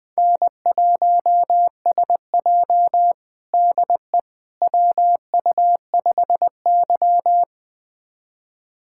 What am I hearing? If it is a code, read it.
Morse code, heard as a keyed tone, about 21 wpm, 700 Hz. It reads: N1SJ DE WU5Y
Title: USA Callsign DE Callsign - 4 Characters - 20wpm